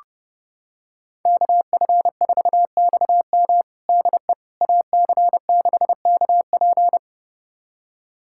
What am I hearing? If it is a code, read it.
Morse code, heard as a keyed tone, about 30 wpm, 700 Hz. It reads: KF4XM DE AC6KP
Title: USA Callsign DE Callsign - 5 Characters - 30wpm